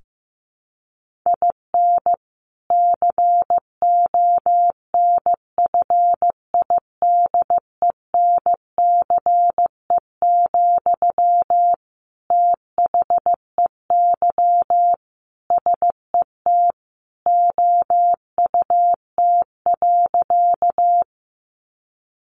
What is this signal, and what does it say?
Morse code, heard as a keyed tone, about 15 wpm, 700 Hz. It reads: IN CONFIDENCE, THEY SET OUT.
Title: Sentences from Top 1500 Words 15wpm